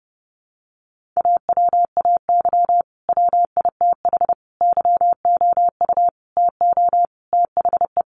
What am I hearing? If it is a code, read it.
Morse code, heard as a keyed tone, about 30 wpm, 700 Hz. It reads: AWAY WITH YOU TO THE
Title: Sentences from Top 1400 Words 30wpm